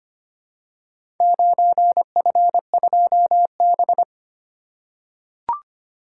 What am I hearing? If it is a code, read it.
Morse code, heard as a keyed tone, about 25 wpm, 700 Hz. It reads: 9F2B
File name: practice / 25wpm / Four Letter-Number Combinations - 25wpm